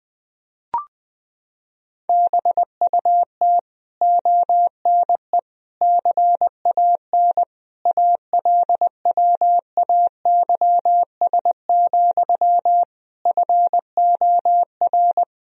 Morse code, heard as a keyed tone, about 20 wpm, 700 Hz. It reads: BUT ONE CAN ALWAYS, FOR
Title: Sentences from Top 1000 Words - Review Part 4 - 20wpm